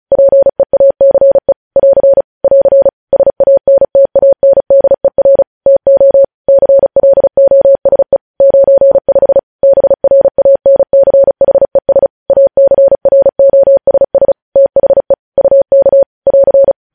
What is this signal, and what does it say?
Morse code, heard as a keyed tone, about 36 wpm, 550 Hz. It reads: PEACE <AR> <AR> SANTANDER TO CLOSE 95 BRANCHES ACROSS THE UK <AR>
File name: podcast / News headlines-1x-35x35